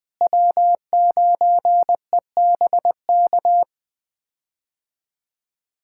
Morse code, heard as a keyed tone, about 20 wpm, 700 Hz. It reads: W9EBK
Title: K1USN SST QSO Practice - 20wpm